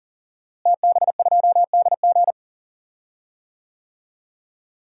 Morse code, heard as a keyed tone, about 40 wpm, 700 Hz. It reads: TB2DG
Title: DXCC-100 Call Signs - 5 Characters - Repeated 2x - 40wpm